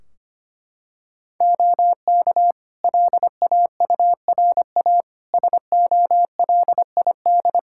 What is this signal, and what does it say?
Morse code, heard as a keyed tone, about 25 wpm, 700 Hz. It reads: OK LAURA SOLID
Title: K1USN SST QSO Practice 25wpm